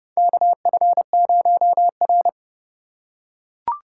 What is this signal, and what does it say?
Morse code, heard as a keyed tone, about 30 wpm, 700 Hz. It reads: KF0R
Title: USA Call Signs - 4 Characters - 3x Character Spacing - 30wpm